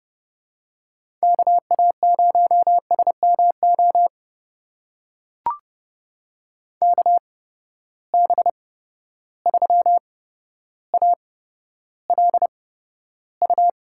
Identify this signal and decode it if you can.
Morse code, heard as a keyed tone, about 30 wpm, spaced as for 13 wpm, 700 Hz. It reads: KA0SMO KB3ALU
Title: USA Call Signs - 6 Characters - 8x Character Spacing - 30wpm